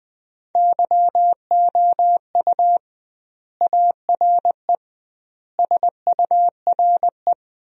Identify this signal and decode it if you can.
Morse code, heard as a keyed tone, about 19 wpm, 700 Hz. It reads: YOU ARE SURE
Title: 2x Word-Spacing - Sentences from Top 300 Words - 20wpm